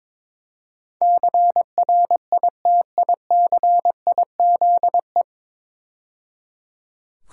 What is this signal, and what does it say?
Morse code, heard as a keyed tone, about 22 wpm, 700 Hz. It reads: CRITICIZE
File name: 9 Letter Words - First 3 Letters Given - 22wpm